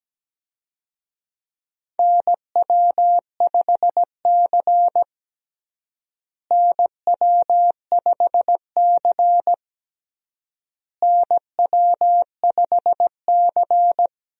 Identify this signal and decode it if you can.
Morse code, heard as a keyed tone, about 17 wpm, 700 Hz. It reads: NW5C NW5C NW5C
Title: USA Call Signs - 4 Characters - Repeated 3x - 17wpm